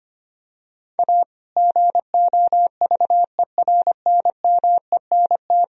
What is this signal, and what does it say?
Morse code, heard as a keyed tone, about 25 wpm, 700 Hz. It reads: A GOVERNMENT
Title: Sentences from Top 1000 Words - Review Part 3 - 25wpm